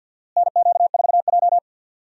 Morse code, heard as a keyed tone, about 52 wpm, 700 Hz. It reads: NQ4J
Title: USA Call Signs - 4 Characters - Repeated 3x - 50wpm